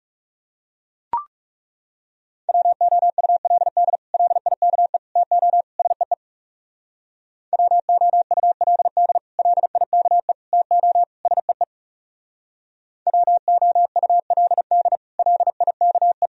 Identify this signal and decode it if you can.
Morse code, heard as a keyed tone, about 39 wpm, 700 Hz. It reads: WOULD LIKE TO SEE WOULD LIKE TO SEE WOULD LIKE